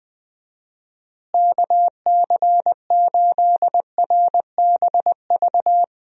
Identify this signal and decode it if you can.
Morse code, heard as a keyed tone, about 20 wpm, 700 Hz. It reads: KC8RBV
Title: USA Call Signs - 6 Characters - Repeated 2x - 20wpm